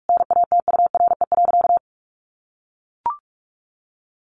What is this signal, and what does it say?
Morse code, heard as a keyed tone, about 44 wpm, 700 Hz. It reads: NATURE.
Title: Sentences from Top 1200 Words 45wpm